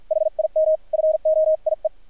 Morse code, heard as a keyed tone, about 50 wpm, spaced as for 43 wpm, 630 Hz. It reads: HIM FOIE
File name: archived / CRSnet